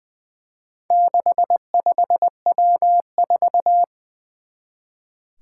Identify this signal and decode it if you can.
Morse code, heard as a keyed tone, about 20 wpm, 700 Hz. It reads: 65W4
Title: Four Challenging Letter-Number Combinations 20wpm